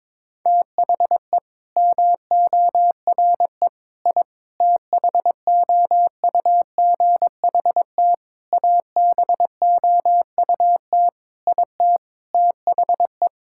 Morse code, heard as a keyed tone, about 22 wpm, 700 Hz. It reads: THE MORE I THOUGHT ABOUT IT THE
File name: Sets of 9 Words - 22wpm